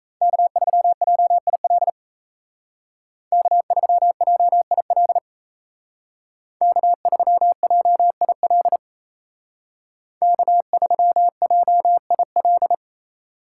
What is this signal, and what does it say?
Morse code, heard as a keyed tone, about 36 wpm, 700 Hz. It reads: K3JIL K3JIL K3JIL K3JIL